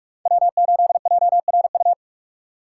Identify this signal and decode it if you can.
Morse code, heard as a keyed tone, about 46 wpm, 700 Hz. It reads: W8JRU